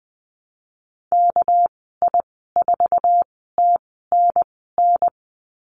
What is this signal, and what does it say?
Morse code, heard as a keyed tone, about 20 wpm, spaced as for 15 wpm, 700 Hz. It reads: KI4TNN
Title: 200 Lesson 32g - Letter k - Call Signs 20wpm